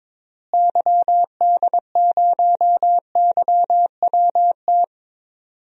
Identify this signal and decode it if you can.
Morse code, heard as a keyed tone, about 22 wpm, 700 Hz. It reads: YD0YWT
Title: DXCC-100 Call Signs - 6 Characters - Repeated 3x - 22wpm